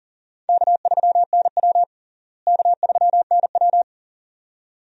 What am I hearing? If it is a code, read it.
Morse code, heard as a keyed tone, about 40 wpm, 700 Hz. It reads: K3NW K3NW
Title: USA Call Signs - 4 Characters - Repeated 2x - 40wpm